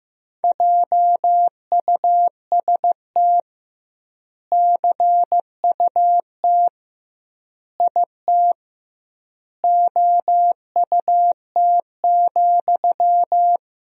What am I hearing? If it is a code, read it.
Morse code, heard as a keyed tone, about 15 wpm, 700 Hz. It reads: JUST CUT IT OUT,
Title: 2x Word-Spacing - Sentences from Top 300 Words - 15wpm